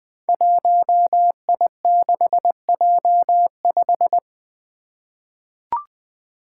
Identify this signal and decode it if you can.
Morse code, heard as a keyed tone, about 20 wpm, 700 Hz. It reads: 1I6J5
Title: Five Challenging Letter-Number Combinations 20wpm